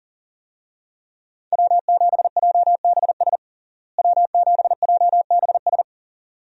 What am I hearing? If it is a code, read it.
Morse code, heard as a keyed tone, about 40 wpm, 700 Hz. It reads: W7JBS W7JBS
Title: USA Call Signs - 5 Characters - Repeated 2x - 40wpm